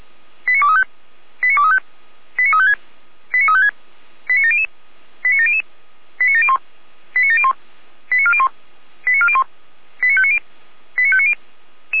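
Five-tone selective call (ZVEI), heard as quick runs of five tones, tones of about 70 ms. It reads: 89235 89235 89356 89356 89700 89700 89701 89701 89401 89401 89590 89590
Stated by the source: ZVEI five-tone call 89126 received by SDR.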